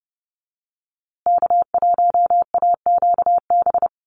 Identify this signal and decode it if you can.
Morse code, heard as a keyed tone, about 30 wpm, 700 Hz. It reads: K1AQB
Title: USA Call Signs Repeated 3x 30wpm